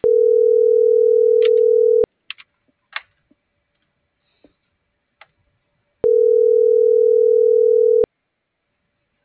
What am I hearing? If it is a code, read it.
Ringback tone (North America), 440 + 480 Hz.